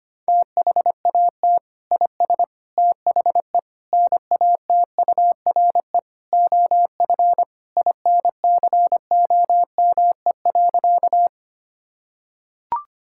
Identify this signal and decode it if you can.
Morse code, heard as a keyed tone, about 25 wpm, 700 Hz. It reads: THAT IS THE NATURE OF INCOME.
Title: Sentences from Top 1000 Words - Review Part 5 - 25wpm